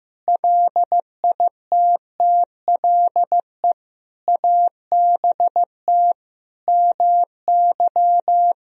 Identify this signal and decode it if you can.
Morse code, heard as a keyed tone, about 16 wpm, 700 Hz. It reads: LITTLE ABT MY
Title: Sentences from Top 400 Words and Abbreviations 15wpm